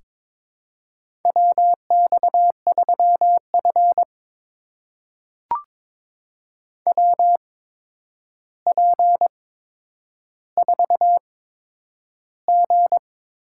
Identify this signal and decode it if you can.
Morse code, heard as a keyed tone, about 22 wpm, spaced as for 9 wpm, 700 Hz. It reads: WX3F WP4G
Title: USA Call Signs - 4 Characters - 8x Character Spacing - 22wpm